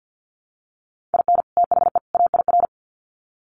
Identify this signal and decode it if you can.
Morse code, heard as a keyed tone, about 50 wpm, 700 Hz. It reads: IN THE AIR
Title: Sets of 3 Words 50wpm